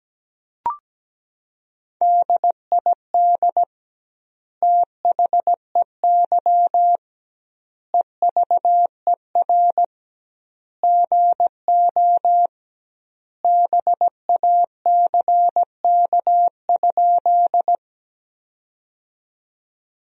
Morse code, heard as a keyed tone, about 17 wpm, 700 Hz. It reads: DID THEY EVER GO BACK?
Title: 2x Word-Spacing - Sentences from Top 400 Words - 17wpm